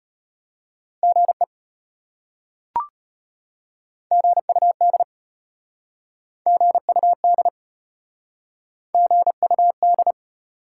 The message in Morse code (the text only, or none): GE GUD GUD GUD